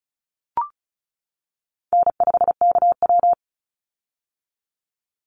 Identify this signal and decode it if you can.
Morse code, heard as a keyed tone, about 36 wpm, 700 Hz. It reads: N5KW
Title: Call Signs - 35wpm